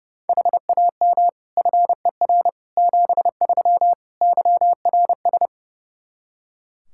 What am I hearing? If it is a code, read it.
Morse code, heard as a keyed tone, about 30 wpm, 700 Hz. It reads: HAM FER 73 YRS
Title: Practice QSO - Exchange 3 - More Info - 30wpm